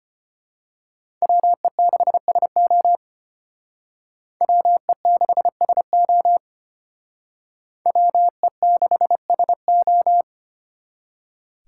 Morse code, heard as a keyed tone, about 29 wpm, 700 Hz. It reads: WE6SO WE6SO WE6SO
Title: Speed Racing - USA Call Signs 25wpm